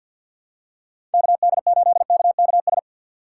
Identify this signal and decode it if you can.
Morse code, heard as a keyed tone, about 51 wpm, 700 Hz. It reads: KD8KKS
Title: USA Call Signs - 6 Characters - 3x Character Spacing - 50wpm